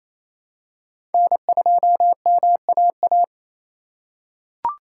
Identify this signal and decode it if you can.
Morse code, heard as a keyed tone, about 28 wpm, 700 Hz. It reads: N2MAA
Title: USA Call Signs - 5 Characters - 4x Character Spacing - 28wpm